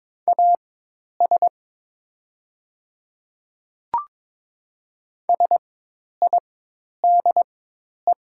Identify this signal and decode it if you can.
Morse code, heard as a keyed tone, about 22 wpm, spaced as for 10 wpm, 700 Hz. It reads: AS SIDE
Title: 4x Character Spacing - Top 300 Words - 22wpm